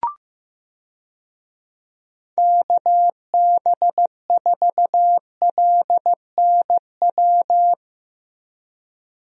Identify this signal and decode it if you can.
Morse code, heard as a keyed tone, about 15 wpm, 700 Hz. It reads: KB4LNW
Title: USA Call Signs - 6 Characters - Repeated 2x - 15wpm